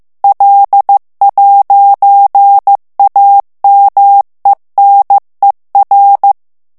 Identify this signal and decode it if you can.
Morse code, heard as a keyed tone, about 15 wpm, 800 Hz. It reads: L'AMENER
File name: FRANCAIS TEXTE CLAIR 2 VITESSE 14 MOTS MINUTES